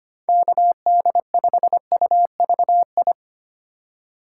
Morse code, heard as a keyed tone, about 25 wpm, 700 Hz. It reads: KD5UVI
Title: USA Call Signs - 6 Characters - 3x Character Spacing - 25wpm